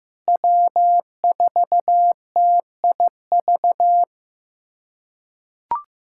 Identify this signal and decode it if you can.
Morse code, heard as a keyed tone, about 15 wpm, 700 Hz. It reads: W4TIV
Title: USA Callsign DE Callsign 15wpm